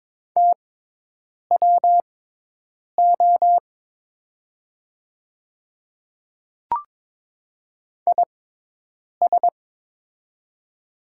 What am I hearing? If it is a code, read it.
Morse code, heard as a keyed tone, about 22 wpm, spaced as for 8 wpm, 700 Hz. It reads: TWO IS
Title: 6x Character Spacing - Top 100 Words - 22wpm